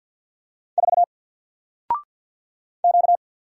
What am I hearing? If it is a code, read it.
Morse code, heard as a keyed tone, about 50 wpm, 700 Hz. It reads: 4 Q
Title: Single Character (Letter:Number:Punctuation:Prosign) Alt 50wpm